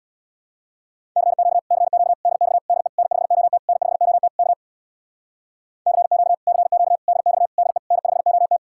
Morse code, heard as a keyed tone, about 64 wpm, 700 Hz. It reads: CQ CQ DX DE N5OT N5OT K CQ CQ DX DE N5OT